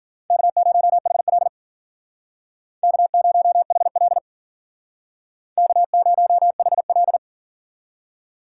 Morse code, heard as a keyed tone, about 50 wpm, 700 Hz. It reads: K0HL K0HL K0HL
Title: Speed Racing - USA 1x2 Call Signs 40wpm